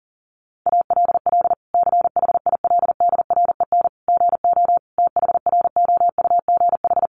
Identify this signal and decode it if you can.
Morse code, heard as a keyed tone, about 40 wpm, 700 Hz. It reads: ALL CHILDREN GO THROUGH
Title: Sentences from Top 1200 Words 40wpm